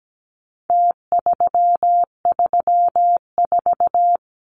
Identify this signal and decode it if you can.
Morse code, heard as a keyed tone, about 17 wpm, 700 Hz. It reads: T334
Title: Numbers - Contest Style - 4-Digit 17wpm